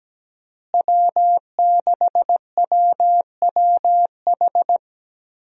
Morse code, heard as a keyed tone, about 17 wpm, 700 Hz. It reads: W6WWH
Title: USA Call Signs - 5 Characters - Repeated 3x - 17wpm